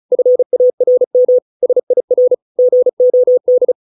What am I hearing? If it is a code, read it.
Morse code, heard as a keyed tone, about 34 wpm, 500 Hz. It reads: FARM SIR GOD